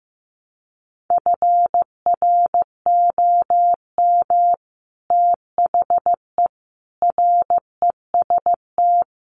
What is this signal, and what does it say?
Morse code, heard as a keyed tone, about 15 wpm, 700 Hz. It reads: FROM THE REST
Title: Sets of 5 Words 15wpm